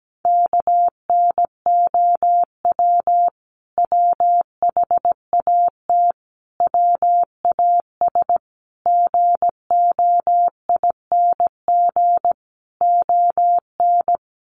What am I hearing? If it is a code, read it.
Morse code, heard as a keyed tone, about 17 wpm, 700 Hz. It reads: KNOW WHAT WAS GOING ON
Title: Sets of 5 Words Encore2 17wpm